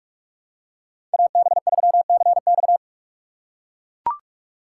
Morse code, heard as a keyed tone, about 41 wpm, 700 Hz. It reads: AB3CX
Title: USA Call Signs - 5 Characters - 4x Character Spacing - 45wpm